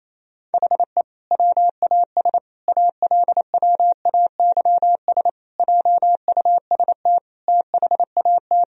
Morse code, heard as a keyed tone, about 28 wpm, 700 Hz. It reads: HE WAS ALWAYS JUST THAT
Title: Sentences from Top 1000 Words - Review Part 4 - 28wpm